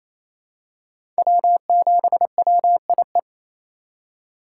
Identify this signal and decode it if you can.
Morse code, heard as a keyed tone, about 28 wpm, 700 Hz. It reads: W7WIE
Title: USA Call Signs - 4x Character Spacing - 28wpm